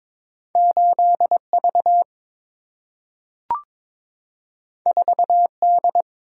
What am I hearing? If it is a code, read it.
Morse code, heard as a keyed tone, about 22 wpm, 700 Hz. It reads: 8 V 4 D
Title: Single-Letter Single-Number 22wpm